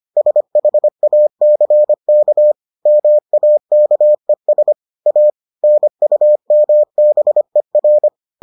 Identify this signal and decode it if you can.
Morse code, heard as a keyed tone, about 25 wpm, 600 Hz. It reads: SHACK MAKES A NUMBER